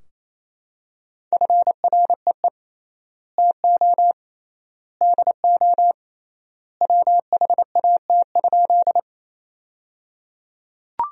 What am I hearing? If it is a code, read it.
Morse code, heard as a keyed tone, about 28 wpm, 700 Hz. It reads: FREE TO DO WHAT?
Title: 3x Word-Spacing - Sentences from Top 500 Words - 28wpm